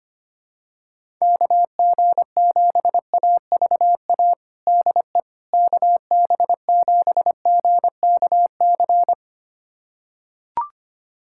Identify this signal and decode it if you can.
Morse code, heard as a keyed tone, about 25 wpm, 700 Hz. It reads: KG7AVA DE KB7GKC
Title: USA Callsign DE Callsign - 6 Characters - 25wpm